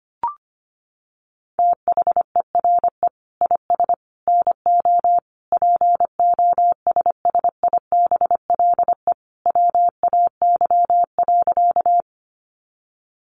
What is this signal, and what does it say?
Morse code, heard as a keyed tone, about 25 wpm, 700 Hz. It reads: THERE IS NO POSSIBLE WAY.
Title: Sentences from Top 500 Words 25wpm